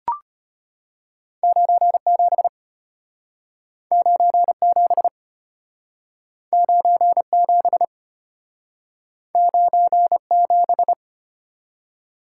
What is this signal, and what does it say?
Morse code, heard as a keyed tone, about 33 wpm, 700 Hz. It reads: 97 97 97 97